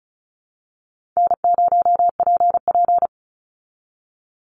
Morse code, heard as a keyed tone, about 34 wpm, 700 Hz. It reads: N0PP
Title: Easy Call Signs Repeated 2x - 35wpm